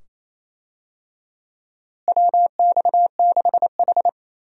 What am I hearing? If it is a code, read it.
Morse code, heard as a keyed tone, about 28 wpm, 700 Hz. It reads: WX6H
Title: USA Call Signs - 4 Characters - Repeated 2x - 28wpm